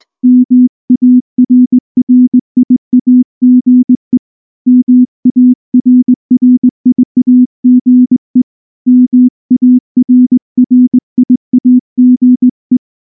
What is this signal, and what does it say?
Morse code, heard as a keyed tone, about 19 wpm, 250 Hz. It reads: MARRIAGE MARRIAGE MARRIAGE
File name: morse-training-20WPM